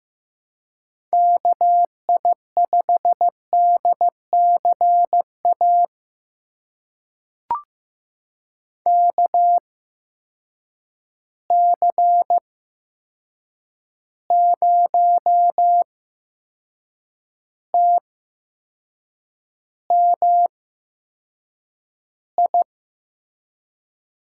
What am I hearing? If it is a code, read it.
Morse code, heard as a keyed tone, about 15 wpm, spaced as for 6 wpm, 700 Hz. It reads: KI5DCA KC0TMI
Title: USA Call Signs - 8x Character Spacing - 15wpm